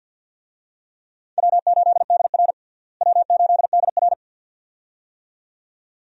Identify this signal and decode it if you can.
Morse code, heard as a keyed tone, about 52 wpm, 700 Hz. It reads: W8DR W8DR
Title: USA Call Signs - 4 Characters - Repeated 2x - 50wpm